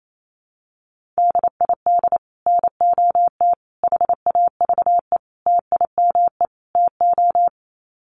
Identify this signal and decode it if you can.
Morse code, heard as a keyed tone, about 28 wpm, 700 Hz. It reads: DID NOT HAVE TIME TO
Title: Sets of 5 Words Encore 28wpm